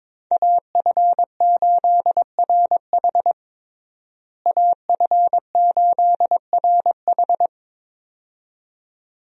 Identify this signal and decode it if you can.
Morse code, heard as a keyed tone, about 22 wpm, 700 Hz. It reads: AF8RH AF8RH
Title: USA Call Signs - 5 Characters - Repeated 2x - 22wpm